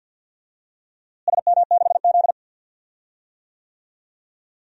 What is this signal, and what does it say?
Morse code, heard as a keyed tone, about 52 wpm, 700 Hz. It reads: SM6Z
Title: DXCC-100 Call Signs - 4 Characters - Repeated 2x - 50wpm